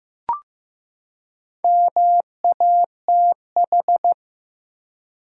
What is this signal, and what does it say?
Morse code, heard as a keyed tone, about 15 wpm, 700 Hz. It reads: MATH
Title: Four Letter Words - 15wpm